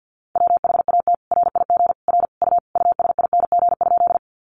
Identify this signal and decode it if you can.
Morse code, heard as a keyed tone, about 50 wpm, 700 Hz. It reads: WHAT RIG R U USING?
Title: Common QSO Phrases 50wpm